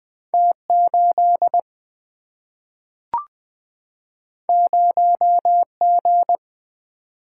Morse code, heard as a keyed tone, about 20 wpm, 700 Hz. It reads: T 8 0 G
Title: Single-Letter Single-Number 20wpm